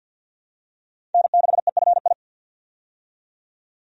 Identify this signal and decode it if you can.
Morse code, heard as a keyed tone, about 47 wpm, 700 Hz. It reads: N6EFI
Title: USA Call Signs - 5 Characters - 50wpm